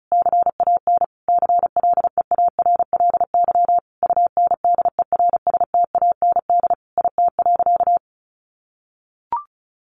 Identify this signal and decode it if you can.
Morse code, heard as a keyed tone, about 34 wpm, 700 Hz. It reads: CAN CLEARLY UNDERSTAND IT.
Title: Sentences from Top 1400 Words 35wpm